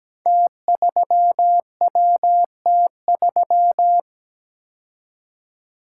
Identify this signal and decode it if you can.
Morse code, heard as a keyed tone, about 17 wpm, 700 Hz. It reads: T3WT3
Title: Code Groups - 5 Letter:Numbers - 17wpm